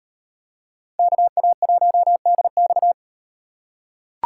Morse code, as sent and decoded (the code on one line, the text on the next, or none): -.- .- .---- -.. -..-
KA1DX